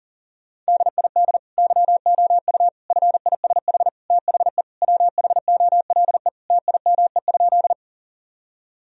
Morse code, heard as a keyed tone, about 40 wpm, 700 Hz. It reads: DID YOU FISH THE WHOLE TIME?
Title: Sentences from Top 1000 Words - Review Part 1 - 40wpm